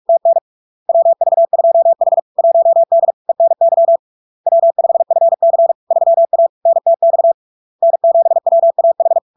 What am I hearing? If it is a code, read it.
Morse code, heard as a keyed tone, about 46 wpm, 670 Hz. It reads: TN WV2H 1D ENY W5FC 3A NTX N7WAH
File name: LARK-FD-2022-Log-in-CW-45wpm-670-Hz